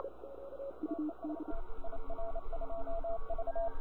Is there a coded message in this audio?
Yes: Morse code from 0.9 to 3.7 s, heard as a keyed tone, about 28 wpm, 670 Hz, with radio noise. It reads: TEST IR2V